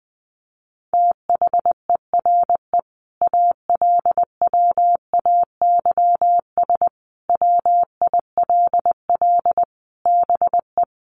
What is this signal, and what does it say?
Morse code, heard as a keyed tone, about 20 wpm, 700 Hz. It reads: THERE ALWAYS WILL BE
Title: Sentences from Top 1100 Words 20wpm